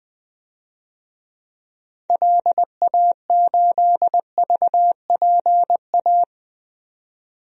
Morse code, heard as a keyed tone, about 20 wpm, 700 Hz. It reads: LA8VPA
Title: DXCC-100 Call Signs - 6 Characters - 20wpm